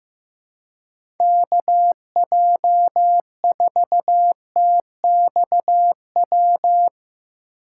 Morse code, heard as a keyed tone, about 15 wpm, 700 Hz. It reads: KJ4TXW
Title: USA Call Signs - 6 Characters - 6x Character Spacing - 15wpm